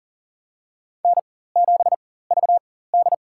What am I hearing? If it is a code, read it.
Morse code, heard as a keyed tone, about 40 wpm, spaced as for 19 wpm, 700 Hz. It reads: N7VD
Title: USA Call Signs - 4 Characters - 4x Character Spacing - 40wpm